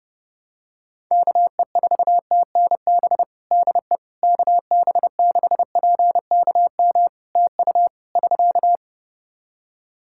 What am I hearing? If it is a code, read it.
Morse code, heard as a keyed tone, about 32 wpm, 700 Hz. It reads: KE4TNB DE KB6PKM TU <SK>
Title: Practice QSO - Exchange 4 - Ending - 30wpm